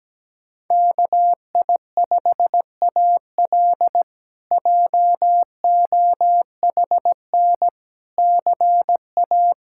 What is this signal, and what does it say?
Morse code, heard as a keyed tone, about 17 wpm, 700 Hz. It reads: KI5AL JOHN CA
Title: K1USN SST QSO Practice - 17wpm